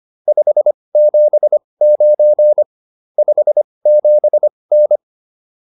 Morse code, heard as a keyed tone, about 25 wpm, spaced as for 20 wpm, 600 Hz. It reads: 579 57N